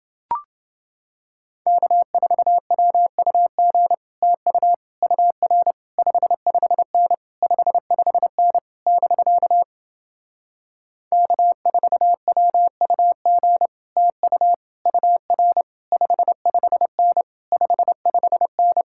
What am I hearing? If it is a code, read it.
Morse code, heard as a keyed tone, about 30 wpm, 700 Hz. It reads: K4WUG TU UR 55N 55N <BK> K4WUG TU UR 55N 55N